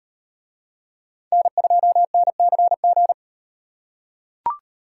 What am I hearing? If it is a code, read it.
Morse code, heard as a keyed tone, about 38 wpm, 700 Hz. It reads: N2NCG